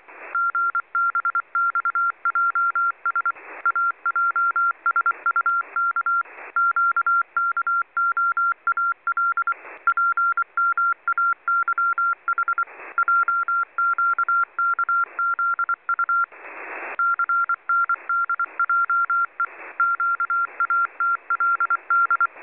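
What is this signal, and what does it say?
Morse code, heard as a keyed tone, about 24 wpm, 1380 Hz. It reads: GBXJS AJSUK QKOAL PMAYH JQKZU CNDJE QATLD
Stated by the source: Morse from numbers stations listening discussion.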